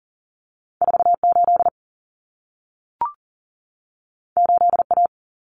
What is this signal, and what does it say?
Morse code, heard as a keyed tone, about 40 wpm, 700 Hz. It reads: 48 8A
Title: Common Cut Numbers - 2-Digit 40wpm